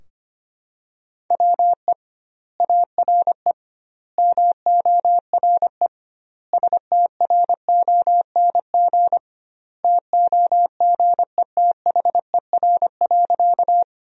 Morse code, heard as a keyed tone, about 25 wpm, 700 Hz. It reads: WE ARE MORE STRONG TOGETHER.
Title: 2x Word-Spacing - Sentences from Top 500 Words - 25wpm